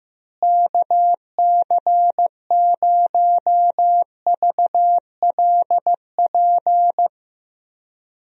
Morse code, heard as a keyed tone, about 15 wpm, 700 Hz. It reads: KC0VLP
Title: USA Call Signs - 6 Characters - Repeated 2x - 15wpm